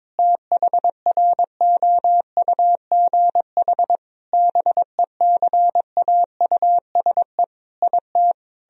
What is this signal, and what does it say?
Morse code, heard as a keyed tone, about 22 wpm, 700 Hz. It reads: THROUGH BECAUSE IT
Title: Sentences from Top 1000 Words - Review Part 1 - 22wpm